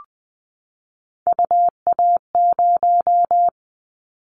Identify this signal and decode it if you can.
Morse code, heard as a keyed tone, about 20 wpm, 700 Hz. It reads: UA0
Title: Callsign Prefixes for Top 100 DXCC - 20wpm